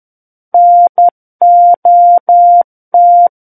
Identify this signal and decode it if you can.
Morse code, heard as a keyed tone, about 11 wpm, 700 Hz. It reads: NOT
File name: Book One / 02-Chapter 2 - 11 WPM